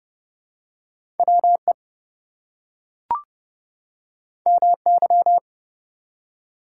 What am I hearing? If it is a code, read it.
Morse code, heard as a keyed tone, about 30 wpm, 700 Hz. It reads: WE MY
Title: Two Letter Words - 30wpm